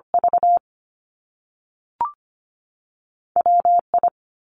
Morse code, heard as a keyed tone, about 25 wpm, 700 Hz. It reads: V WI
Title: Top Two Letter Combinations in Words 25wpm